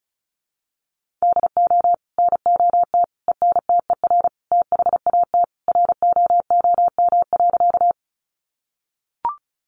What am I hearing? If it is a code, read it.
Morse code, heard as a keyed tone, about 34 wpm, 700 Hz. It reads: DO NOT ENTER THAT ROOM.
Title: Sentences from Top 1300 Words 35wpm